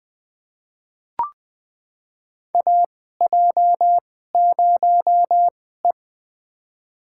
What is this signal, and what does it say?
Morse code, heard as a keyed tone, about 21 wpm, spaced as for 15 wpm, 700 Hz. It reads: AJ0E
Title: USA Call Signs - 2x Character Spacing - 20wpm